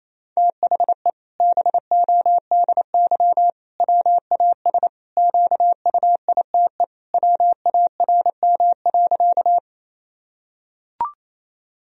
Morse code, heard as a keyed tone, about 27 wpm, 700 Hz. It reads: THE BODY WAS QUITE WARM.
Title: Sentences from Top 1000 Words - Review Part 2 - 28wpm